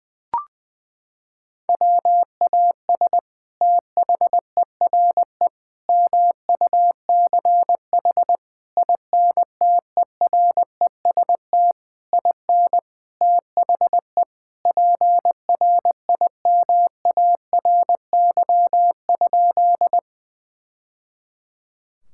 Morse code, heard as a keyed tone, about 20 wpm, 700 Hz. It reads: WAS THERE MUCH INTEREST IN THE PRIMARY?
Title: Sentences from Top 1000 Words - Review Part 5 - 20wpm